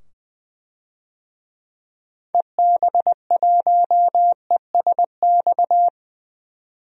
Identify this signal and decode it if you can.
Morse code, heard as a keyed tone, about 21 wpm, 700 Hz. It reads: EB1ESX
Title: DXCC-100 Call Signs - 6 Characters - 20wpm